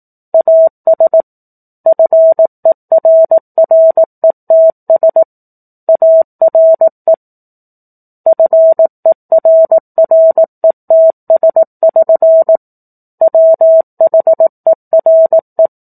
Morse code, heard as a keyed tone, about 18 wpm, 650 Hz. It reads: AS FERRETS ARE FERRETS<SN> WHERE
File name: 04-Chapter-04 18 WPM